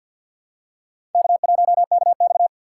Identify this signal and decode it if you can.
Morse code, heard as a keyed tone, about 52 wpm, 700 Hz. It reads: K1KX
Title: USA Call Signs - 6x Character Spacing - 50wpm